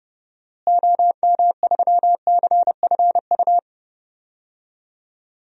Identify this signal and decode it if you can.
Morse code, heard as a keyed tone, about 32 wpm, 700 Hz. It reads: OM3CFU
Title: DXCC-100 Call Signs - 6 Characters - Repeated 3x - 30wpm